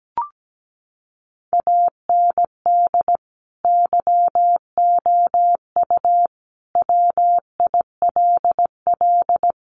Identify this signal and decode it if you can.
Morse code, heard as a keyed tone, about 17 wpm, 700 Hz. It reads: AND YOU WILL
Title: Sentences from Top 400 Words 17wpm